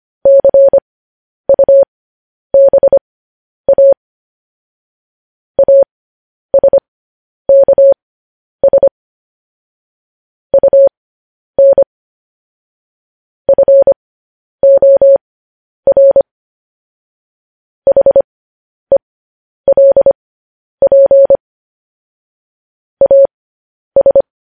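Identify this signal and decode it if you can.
Morse code, heard as a keyed tone, about 25 wpm, spaced as for 10 wpm, 550 Hz. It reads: CUBA ASKS UN FOR HELP AS
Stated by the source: Morse code transmission of news headlines.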